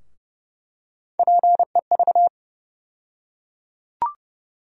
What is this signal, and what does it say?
Morse code, heard as a keyed tone, about 30 wpm, 700 Hz. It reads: PEV
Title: USA Call Sign Postfixes - 3 Letters 30wpm